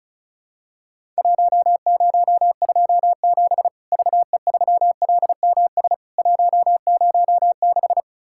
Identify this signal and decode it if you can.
Morse code, heard as a keyed tone, about 36 wpm, 700 Hz. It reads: 1027 VE3LMS 106
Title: Worked All Europe DX Contest - QTC Exchange - 35wpm